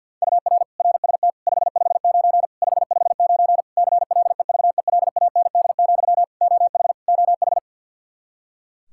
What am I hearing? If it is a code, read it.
Morse code, heard as a keyed tone, about 52 wpm, 700 Hz. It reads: UR RST 559 559 CLEVELAND, OH OH
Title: Practice QSO - Exchange 1 - Beginning - 50wpm